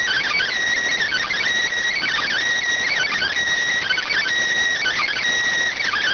Emergency Alert System (SAME) tones, heard as harsh buzzing bursts. It is heard from 1.0 to 5.2 s.